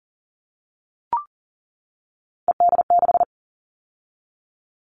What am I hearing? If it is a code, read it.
Morse code, heard as a keyed tone, about 40 wpm, 700 Hz. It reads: ED6
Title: Callsign Prefixes for Top 50 DXCC - 40wpm